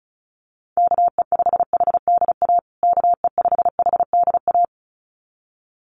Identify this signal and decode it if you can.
Morse code, heard as a keyed tone, about 34 wpm, 700 Hz. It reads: KE5HDA KE5HDA
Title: USA Call Signs Repeated 2x 35wpm